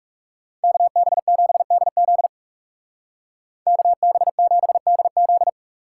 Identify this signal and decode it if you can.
Morse code, heard as a keyed tone, about 36 wpm, 700 Hz. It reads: KB7DZ KB7DZ
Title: Speed Racing - USA 2x2 Call Signs 30wpm